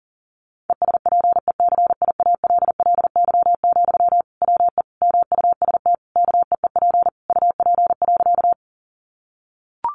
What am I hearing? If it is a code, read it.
Morse code, heard as a keyed tone, about 40 wpm, 700 Hz. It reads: ESPECIALLY, WE MUST KEEP UP.
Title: Sentences from Top 700 Words 40wpm